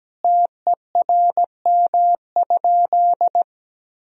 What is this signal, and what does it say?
Morse code, heard as a keyed tone, about 17 wpm, 700 Hz. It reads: TERM?
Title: Sentences from Top 1000 Words - Review Part 4 - 17wpm